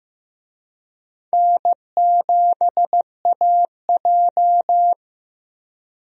Morse code, heard as a keyed tone, about 15 wpm, 700 Hz. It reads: N7AJ
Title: USA Call Signs - 4 Characters - 15wpm